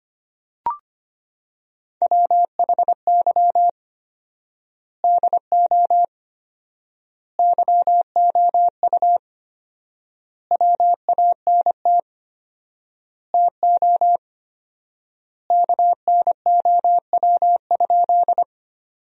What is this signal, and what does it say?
Morse code, heard as a keyed tone, about 25 wpm, 700 Hz. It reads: WHY DO YOU WANT TO KNOW?
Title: 4x Word-Spacing - Sentences from Top 200 Words - 25wpm